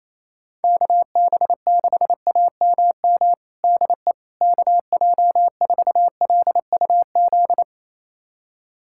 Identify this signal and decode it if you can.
Morse code, heard as a keyed tone, about 27 wpm, 700 Hz. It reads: KB6AMM DE KJ4LUZ
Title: USA Callsign DE Callsign - 6 Characters - 28wpm